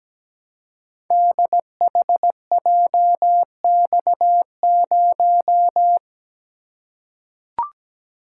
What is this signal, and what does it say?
Morse code, heard as a keyed tone, about 17 wpm, 700 Hz. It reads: DHJX0
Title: Code Groups - 5 Letter:Numbers - 17wpm